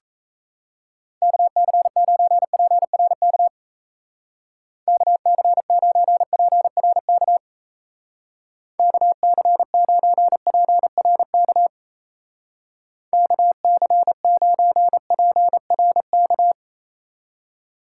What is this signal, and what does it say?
Morse code, heard as a keyed tone, about 35 wpm, 700 Hz. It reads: KC9PRK KC9PRK KC9PRK KC9PRK